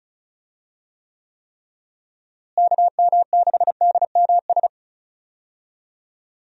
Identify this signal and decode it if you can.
Morse code, heard as a keyed tone, about 36 wpm, 700 Hz. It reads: KM6DMS
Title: USA Call Signs - 6 Characters - Repeated 3x - 35wpm